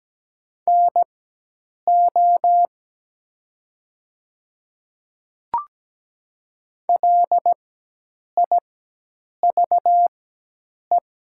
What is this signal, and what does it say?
Morse code, heard as a keyed tone, about 17 wpm, spaced as for 8 wpm, 700 Hz. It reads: NO LIVE
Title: 4x Character Spacing - Top 200 Words - 17wpm